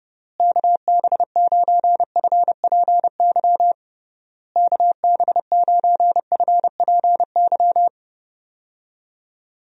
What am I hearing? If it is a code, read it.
Morse code, heard as a keyed tone, about 30 wpm, 700 Hz. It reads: KB9FPY KB9FPY
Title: USA Call Signs - 6 Characters - Repeated 3x - 30wpm